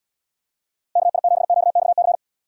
Morse code, heard as a keyed tone, about 77 wpm, 700 Hz. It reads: KI7QXC